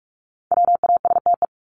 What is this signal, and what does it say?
Morse code, heard as a keyed tone, about 44 wpm, 700 Hz. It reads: WASTE
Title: Top 1751-2000 Words 45wpm